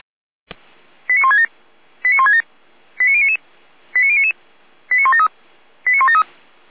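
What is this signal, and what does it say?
Five-tone selective call (ZVEI), heard as quick runs of five tones, tones of about 80 ms. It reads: 89167 89167 89099 89099 89173 89173
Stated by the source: ZVEI five-tone selective call received by SDR.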